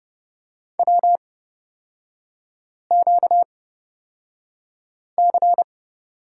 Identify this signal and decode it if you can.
Morse code, heard as a keyed tone, about 29 wpm, spaced as for 5 wpm, 700 Hz. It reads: WQC
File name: Lesson 38e - Letter q - Three Characters 30wpm